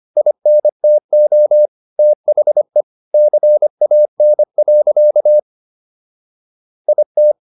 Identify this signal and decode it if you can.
Morse code, heard as a keyed tone, about 25 wpm, 600 Hz. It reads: INTO THE CAN. IT